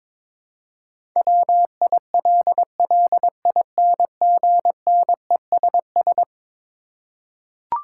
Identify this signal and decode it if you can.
Morse code, heard as a keyed tone, about 22 wpm, 700 Hz. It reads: WILLINGNESS
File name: practice / 22wpm / Eleven Letter Words - 22wpm